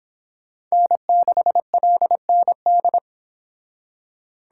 Morse code, heard as a keyed tone, about 25 wpm, 700 Hz. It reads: N6LND